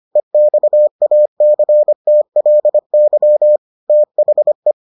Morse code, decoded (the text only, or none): EXACTLY THE